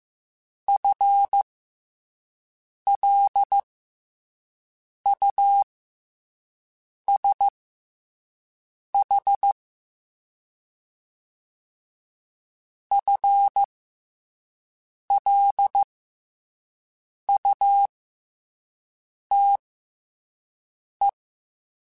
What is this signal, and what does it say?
Morse code, heard as a keyed tone, about 15 wpm, spaced as for 5 wpm, 800 Hz. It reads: FLUSH FLUTE
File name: FL Runs / FL Runs mp3s / FL Runs @5